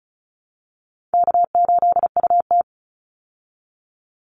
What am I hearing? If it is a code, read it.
Morse code, heard as a keyed tone, about 34 wpm, 700 Hz. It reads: K8UT
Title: Easy Call Signs Repeated 2x - 35wpm